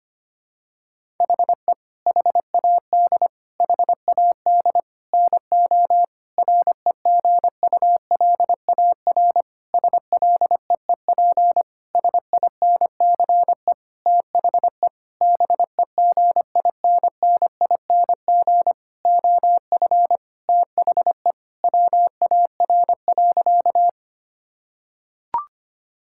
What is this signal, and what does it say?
Morse code, heard as a keyed tone, about 25 wpm, 700 Hz. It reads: HE HAD HAD NO REGULAR SLEEP SINCE THE BEGINNING OF THE WAR.
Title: Sentences from Top 1000 Words - Review Part 6 - 25wpm